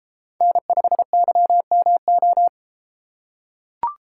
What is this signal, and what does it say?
Morse code, heard as a keyed tone, about 33 wpm, 700 Hz. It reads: N5YMO